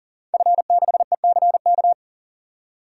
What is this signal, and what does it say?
Morse code, heard as a keyed tone, about 40 wpm, 700 Hz. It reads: F6ECK
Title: DXCC-100 Call Signs - 5 Characters - 40wpm